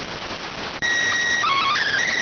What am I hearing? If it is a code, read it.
SSTV (slow-scan television), PD 290.